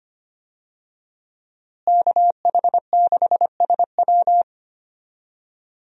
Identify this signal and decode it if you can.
Morse code, heard as a keyed tone, about 25 wpm, 700 Hz. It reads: KH6SW
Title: DXCC-100 Call Signs - 5 Characters - Repeated 2x - 25wpm